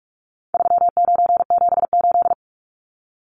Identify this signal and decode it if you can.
Morse code, heard as a keyed tone, about 44 wpm, 700 Hz. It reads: 3978
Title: Numbers - 4-Digit 45wpm